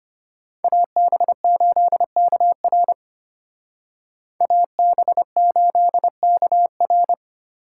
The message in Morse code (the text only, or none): AB8KR AB8KR